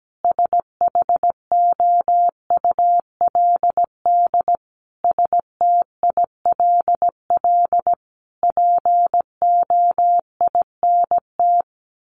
Morse code, heard as a keyed tone, about 17 wpm, 700 Hz. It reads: SHOULD STILL POINT
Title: Sentences from Top 1500 Words 17wpm